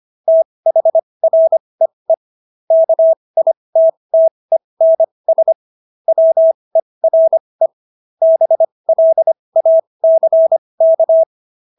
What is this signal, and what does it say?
Morse code, heard as a keyed tone, about 25 wpm, spaced as for 20 wpm, 650 Hz. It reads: THREE KITTENS WERE BLACK